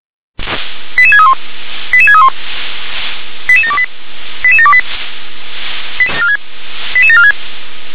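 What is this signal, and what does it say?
Five-tone selective call (ZVEI), heard as quick runs of five tones, tones of about 70 ms. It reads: 88631 88631 88728 88728 88746
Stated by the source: ZVEI five-tone selective call received by SDR.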